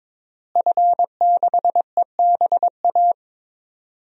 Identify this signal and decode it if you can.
Morse code, heard as a keyed tone, about 22 wpm, 700 Hz. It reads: F6EBA
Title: DXCC-100 Call Signs - 5 Characters - Repeated 3x - 22wpm